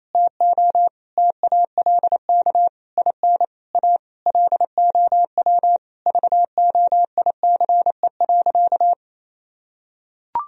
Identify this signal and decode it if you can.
Morse code, heard as a keyed tone, about 28 wpm, 700 Hz. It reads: TO TALK IN A LOW VOICE.
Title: Sentences from Top 1000 Words - Review Part 1 - 28wpm